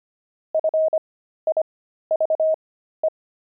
Morse code, heard as a keyed tone, about 25 wpm, spaced as for 13 wpm, 630 Hz. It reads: FIVE
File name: pangram-fs-13-cs-25-wizards